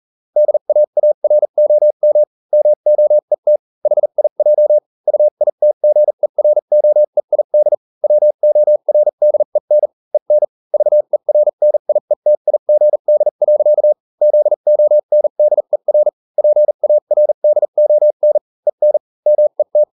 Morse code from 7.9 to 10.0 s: WORDEN